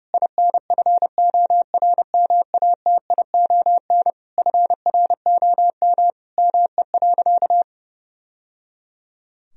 Morse code, heard as a keyed tone, about 29 wpm, 700 Hz. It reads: INFORMATION FROM ME.
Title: Sentences from Top 1000 Words - Review Part 3 - 30wpm